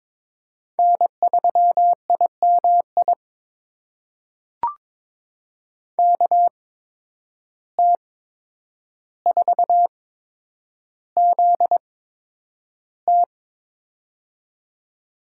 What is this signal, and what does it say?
Morse code, heard as a keyed tone, about 22 wpm, spaced as for 9 wpm, 700 Hz. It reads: N3IMI KT4ZT
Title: USA Call Signs - 5 Characters - 8x Character Spacing - 22wpm